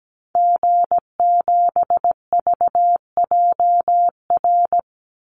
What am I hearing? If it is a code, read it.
Morse code, heard as a keyed tone, about 17 wpm, 700 Hz. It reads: G7VJR
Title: Call Signs Repeated 2x - 17wpm